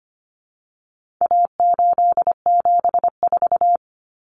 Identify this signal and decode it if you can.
Morse code, heard as a keyed tone, about 25 wpm, 700 Hz. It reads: A874
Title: Common Cut Numbers - 4-Digit 25wpm